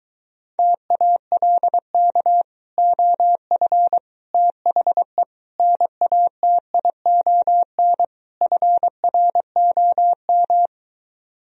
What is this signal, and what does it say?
Morse code, heard as a keyed tone, about 22 wpm, 700 Hz. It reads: TALK OF THE NATION FROM